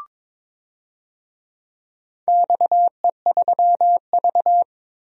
Morse code, heard as a keyed tone, about 22 wpm, 700 Hz. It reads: XE3V
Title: DXCC-100 Call Signs - 4 Characters - Repeated 2x - 22wpm